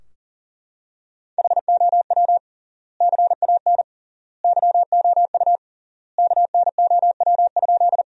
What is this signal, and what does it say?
Morse code, heard as a keyed tone, about 37 wpm, 700 Hz. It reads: HOW CAN YOU KNOW?
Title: 3x Word-Spacing - Sentences from Top 100 Words - 40wpm